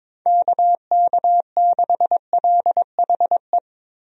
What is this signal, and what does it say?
Morse code, heard as a keyed tone, about 22 wpm, 700 Hz. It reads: KK6LHE
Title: USA Call Signs - 6 Characters - Repeated 2x - 22wpm